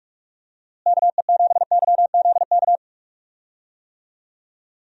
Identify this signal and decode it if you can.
Morse code, heard as a keyed tone, about 46 wpm, 700 Hz. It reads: KE7YZK
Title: USA Call Signs - 6 Characters - Repeated 2x - 45wpm